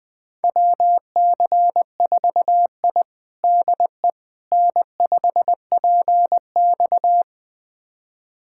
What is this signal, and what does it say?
Morse code, heard as a keyed tone, about 20 wpm, 700 Hz. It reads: WC4I DE N5PX
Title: USA Callsign DE Callsign - 4 Characters - 20wpm